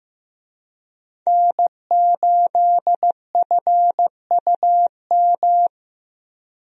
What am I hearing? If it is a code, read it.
Morse code, heard as a keyed tone, about 16 wpm, 700 Hz. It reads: N8FUM
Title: USA Call Signs - 5 Characters - Repeated 2x - 15wpm